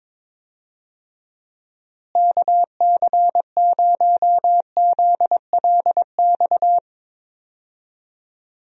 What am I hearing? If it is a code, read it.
Morse code, heard as a keyed tone, about 21 wpm, 700 Hz. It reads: KC0ZLX
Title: USA Call Signs - 6 Characters - Repeated 2x - 22wpm